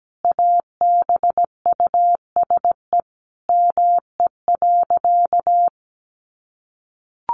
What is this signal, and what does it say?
Morse code, heard as a keyed tone, about 17 wpm, 700 Hz. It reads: ABUSE ME.
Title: Sentences from Top 1300 Words 17wpm